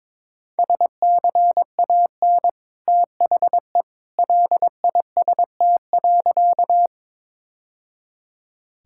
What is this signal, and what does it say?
Morse code, heard as a keyed tone, about 22 wpm, 700 Hz. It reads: SCAN THE LIST.
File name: practice / 22wpm / Easy Sentences with Hints - 22wpm